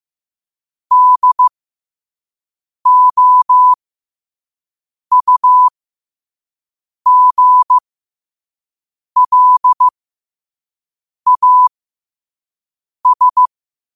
Morse code, heard as a keyed tone, about 15 wpm, spaced as for 5 wpm, 1000 Hz. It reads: DOUGLAS
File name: qotdfiles / 5wpm